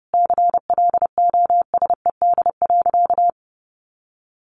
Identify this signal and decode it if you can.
Morse code, heard as a keyed tone, about 30 wpm, 700 Hz. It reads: CLOSED.
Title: Sentences from Top 1400 Words 30wpm